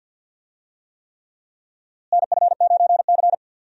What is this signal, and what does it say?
Morse code, heard as a keyed tone, about 52 wpm, 700 Hz. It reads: NF9C
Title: USA Call Signs - 4 Characters - Repeated 3x - 50wpm